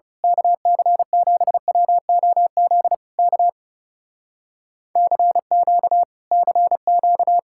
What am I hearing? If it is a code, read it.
Morse code, heard as a keyed tone, about 28 wpm, 700 Hz. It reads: KC7WOZ K CQ CQ
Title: Speed-Racing Common QSO Phrases - 30wpm